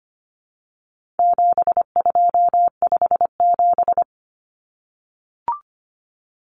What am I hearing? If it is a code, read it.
Morse code, heard as a keyed tone, about 25 wpm, 700 Hz. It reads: 7257
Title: Numbers - 4-Digit 25wpm